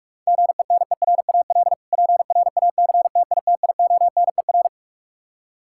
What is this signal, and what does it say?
Morse code, heard as a keyed tone, about 46 wpm, 700 Hz. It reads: GENERAL PRACTITIONER
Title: K1USN SST QSO Practice 45wpm